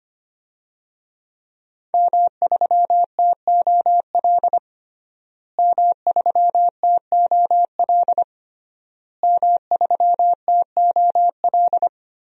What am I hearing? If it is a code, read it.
Morse code, heard as a keyed tone, about 25 wpm, 700 Hz. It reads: M3TOL M3TOL M3TOL
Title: DXCC-100 Call Signs - 5 Characters - Repeated 3x - 25wpm